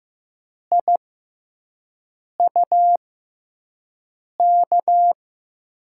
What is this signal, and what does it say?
Morse code, heard as a keyed tone, about 15 wpm, spaced as for 5 wpm, 700 Hz. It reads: IUK
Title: USA Call Signs - 6x Character Spacing - 15wpm